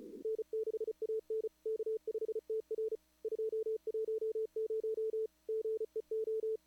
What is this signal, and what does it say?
Morse code, heard as a keyed tone, about 34 wpm, 440 Hz. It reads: N6AN K5TR 210 GEO
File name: CWSprint / Feb2022 / k5tr-feb-na-sprint-cw-k5ot -0100